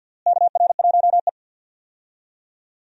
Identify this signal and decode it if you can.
Morse code, heard as a keyed tone, about 53 wpm, 700 Hz. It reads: KR1E
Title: USA Call Signs - 4 Characters - Repeated 2x - 50wpm